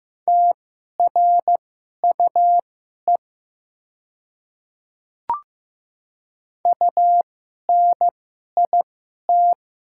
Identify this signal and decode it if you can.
Morse code, heard as a keyed tone, about 15 wpm, spaced as for 11 wpm, 700 Hz. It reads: TRUE UNIT
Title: 2x Character Spacing - Top 500 Words - 15wpm